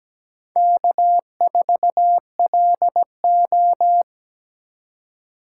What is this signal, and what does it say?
Morse code, heard as a keyed tone, about 17 wpm, 700 Hz. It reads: K4LO
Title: USA Call Signs - 8x Character Spacing - 17wpm